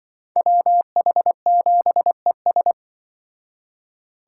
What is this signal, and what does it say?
Morse code, heard as a keyed tone, about 25 wpm, 700 Hz. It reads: WH7ES
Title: Speed Racing - USA 2x2 Call Signs 20wpm